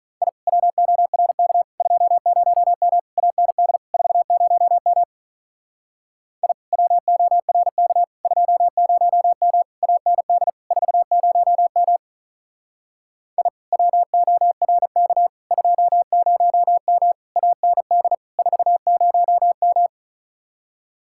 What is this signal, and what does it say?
Morse code, heard as a keyed tone, about 45 wpm, 700 Hz. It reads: I WORK 20M AND 40M I WORK 20M AND 40M I WORK 20M AND 40M